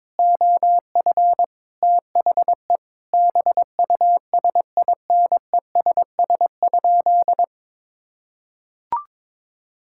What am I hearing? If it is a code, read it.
Morse code, heard as a keyed tone, about 22 wpm, 700 Hz. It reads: OF THE BUSINESS?
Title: Sentences from Top 1600 Words 22wpm